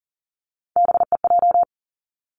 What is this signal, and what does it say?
Morse code, heard as a keyed tone, about 40 wpm, 700 Hz. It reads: BEJ
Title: Three Challenging Letter-Number Combinations 40wpm